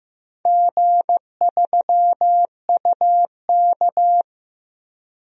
Morse code, heard as a keyed tone, about 15 wpm, 700 Hz. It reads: G3UK
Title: DXCC-100 Call Signs - 4 Characters - 15wpm